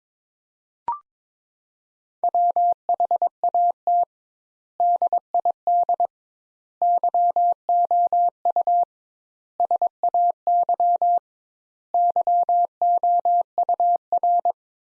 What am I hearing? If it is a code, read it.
Morse code, heard as a keyed tone, about 22 wpm, 700 Hz. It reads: WHAT DID YOU SAY YOUR
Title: Sentences Top 200 Words - 22wpm